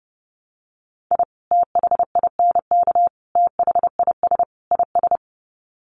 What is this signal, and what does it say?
Morse code, heard as a keyed tone, about 30 wpm, 700 Hz. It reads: I THINK THIS IS
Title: Sets of 4 Words 30wpm